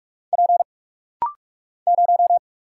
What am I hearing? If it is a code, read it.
Morse code, heard as a keyed tone, about 44 wpm, 700 Hz. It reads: P 0
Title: Single Letter-Number Alt - Rapid-Fire 45wpm